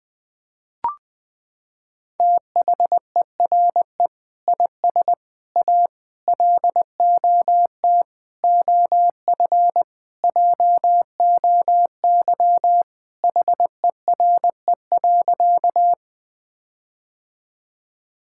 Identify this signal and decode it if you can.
Morse code, heard as a keyed tone, about 20 wpm, 700 Hz. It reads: THERE IS A LOT OF JOY HERE.
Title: Sentences with 5-Word Phrases - 20wpm